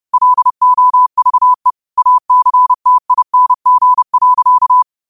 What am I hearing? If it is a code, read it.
Morse code, heard as a keyed tone, about 30 wpm, 1000 Hz. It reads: LOVE ACTING.
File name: qotdfiles / 30wpm